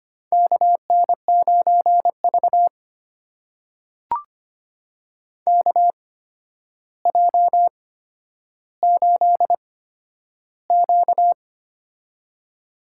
Morse code, heard as a keyed tone, about 25 wpm, spaced as for 11 wpm, 700 Hz. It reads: KN9V KJ8Q
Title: USA Call Signs - 4 Characters - 8x Character Spacing - 25wpm